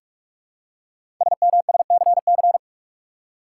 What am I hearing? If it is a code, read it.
Morse code, heard as a keyed tone, about 44 wpm, 700 Hz. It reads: SMSCC
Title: Code Groups - 5 Letters - 45wpm